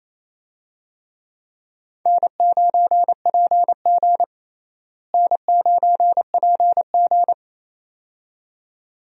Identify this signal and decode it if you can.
Morse code, heard as a keyed tone, about 28 wpm, 700 Hz. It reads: N9PG N9PG
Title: USA Call Signs - 4 Characters - Repeated 2x - 28wpm